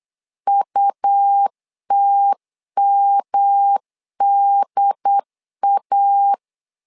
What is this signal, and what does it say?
Morse code, heard as a keyed tone, about 8 wpm, 800 Hz. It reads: UTMDA
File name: LEZIONI / 3 MWRO / 3 M 8wpm 2300